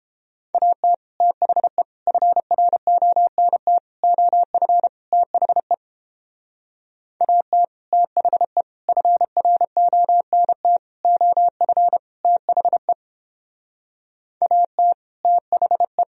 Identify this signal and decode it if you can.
Morse code, heard as a keyed tone, about 29 wpm, 700 Hz. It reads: AT THE FRONT OF THE AT THE FRONT OF THE AT THE